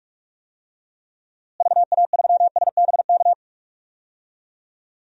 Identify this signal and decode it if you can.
Morse code, heard as a keyed tone, about 46 wpm, 700 Hz. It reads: VA3SBK
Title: DXCC-100 Call Signs - 6 Characters - Repeated 2x - 45wpm